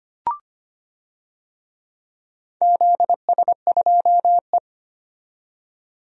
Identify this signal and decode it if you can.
Morse code, heard as a keyed tone, about 25 wpm, 700 Hz. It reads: ZS2E
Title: DXCC-100 Call Signs - 4 Characters - 25wpm